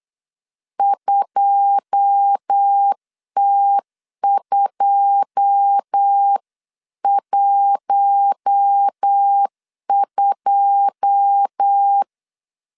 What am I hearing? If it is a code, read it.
Morse code, heard as a keyed tone, about 8 wpm, 800 Hz. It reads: 2T212
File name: LEZIONI / 9 2 8wpm 3437